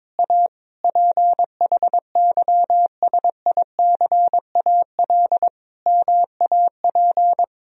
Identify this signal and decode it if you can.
Morse code, heard as a keyed tone, about 21 wpm, 700 Hz. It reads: A PHYSICAL MAP
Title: Sentences from Top 1000 Words - Review Part 4 - 22wpm